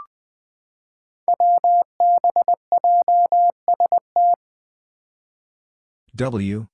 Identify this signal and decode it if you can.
Morse code, heard as a keyed tone, about 20 wpm, 700 Hz. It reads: WBJST
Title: Five Challenging Letter-Number Combinations 20wpm